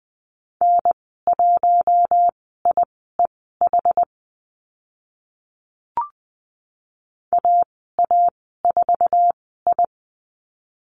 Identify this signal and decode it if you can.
Morse code, heard as a keyed tone, about 20 wpm, spaced as for 15 wpm, 700 Hz. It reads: N1IEH AA4I
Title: 076 Lesson 13g - Letter L - Call Signs 20wpm